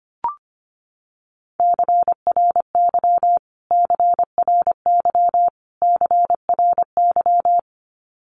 Morse code, heard as a keyed tone, about 25 wpm, 700 Hz. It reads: CRY CRY CRY
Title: Top 500 Words Repeated 25wpm